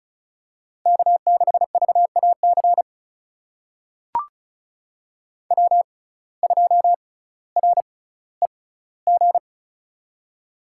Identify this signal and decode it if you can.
Morse code, heard as a keyed tone, about 36 wpm, spaced as for 18 wpm, 700 Hz. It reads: K6VAC W2REG
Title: USA Call Signs - 5 Characters - 6x Character Spacing - 35wpm